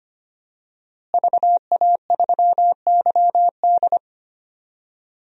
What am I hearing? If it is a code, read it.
Morse code, heard as a keyed tone, about 25 wpm, 700 Hz. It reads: VA3YD
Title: DXCC-100 Call Signs - 5 Characters - Repeated 2x - 25wpm